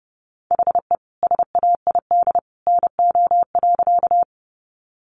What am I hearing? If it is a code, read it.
Morse code, heard as a keyed tone, about 28 wpm, 700 Hz. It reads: HE SAID NO.
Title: Sentences from Top 100 Words 30wpm